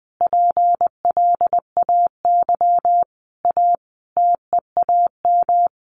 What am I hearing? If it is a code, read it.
Morse code, heard as a keyed tone, about 20 wpm, 700 Hz. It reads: PLAY A TEAM
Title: Sentences from Top 1500 Words 20wpm